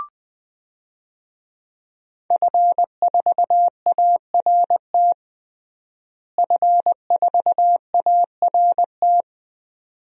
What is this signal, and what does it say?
Morse code, heard as a keyed tone, about 20 wpm, 700 Hz. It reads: F4ART F4ART
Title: DXCC-100 Call Signs - 5 Characters - Repeated 2x - 20wpm